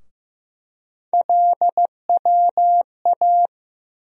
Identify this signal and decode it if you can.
Morse code, heard as a keyed tone, about 15 wpm, 700 Hz. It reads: LWA
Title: USA Call Sign Postfixes - 3 Letters 15wpm